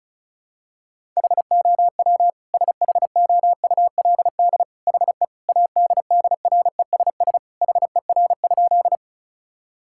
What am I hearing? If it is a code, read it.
Morse code, heard as a keyed tone, about 36 wpm, 700 Hz. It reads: HOW SHOULD HE ADDRESS HER?
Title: Sentences from Top 1000 Words - Review Part 6 - 35wpm